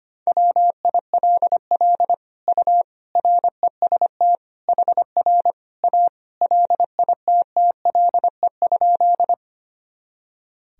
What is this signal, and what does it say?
Morse code, heard as a keyed tone, about 25 wpm, 700 Hz. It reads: WILL U REST HR A LITTLE?
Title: Sentences from Top 500 Words and Abbreviations 25wpm